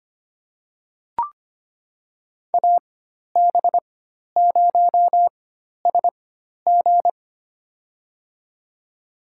Morse code, heard as a keyed tone, about 25 wpm, spaced as for 12 wpm, 700 Hz. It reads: AB0SG
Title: USA Call Signs - 5 Characters - 4x Character Spacing - 25wpm